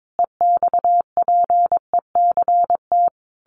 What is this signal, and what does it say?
Morse code, heard as a keyed tone, about 22 wpm, 700 Hz. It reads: EXPECT
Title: Sentences from Top 1400 Words 22wpm